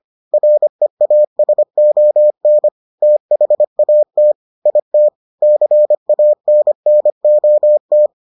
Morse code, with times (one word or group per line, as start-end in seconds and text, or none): 0.3-2.7 REASON
3.0-4.3 THAT
4.7-5.1 IT
5.4-8.1 CANNOT